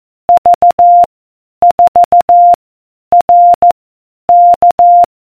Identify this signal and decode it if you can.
Morse code, heard as a keyed tone, about 14 wpm, 700 Hz. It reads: V 4 R K